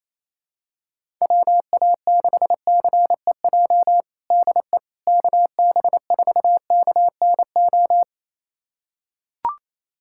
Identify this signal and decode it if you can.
Morse code, heard as a keyed tone, about 28 wpm, 700 Hz. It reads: WA6CEJ DE KB4KNO
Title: USA Callsign DE Callsign - 6 Characters - 28wpm